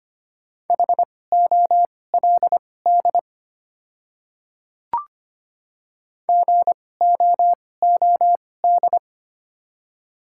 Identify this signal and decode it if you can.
Morse code, heard as a keyed tone, about 25 wpm, spaced as for 18 wpm, 690 Hz. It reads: HOLD GOOD
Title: 2x Character Spacing - Top 500 Words - 25wpm